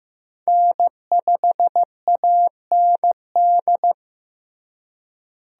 Morse code, heard as a keyed tone, about 15 wpm, 700 Hz. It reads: N5AND
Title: USA Call Signs - 5 Characters - Repeated 2x - 15wpm